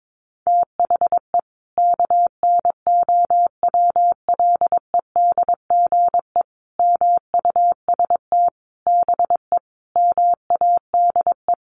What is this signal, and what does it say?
Morse code, heard as a keyed tone, about 22 wpm, 700 Hz. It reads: THE KNOWLEDGE MUST BE MADE
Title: Sentences from Top 1400 Words 22wpm